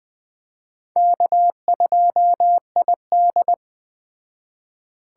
Morse code, heard as a keyed tone, about 21 wpm, 700 Hz. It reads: K2ID